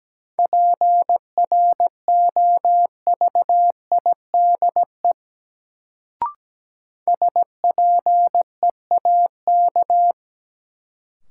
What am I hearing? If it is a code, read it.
Morse code, heard as a keyed tone, about 17 wpm, 700 Hz. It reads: PROVIDE SPEAK
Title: Sentences from Top 1100 Words - New Word Drill Alt - 17wpm